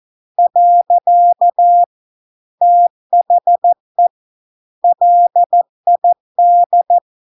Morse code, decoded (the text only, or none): . THE LID